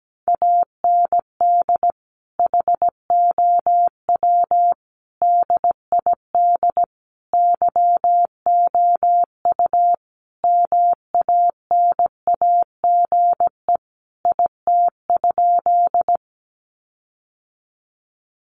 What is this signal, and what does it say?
Morse code, heard as a keyed tone, about 17 wpm, 700 Hz. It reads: AND HOW DID YOU MANAGE IT?
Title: Sentences from Top 1400 Words 17wpm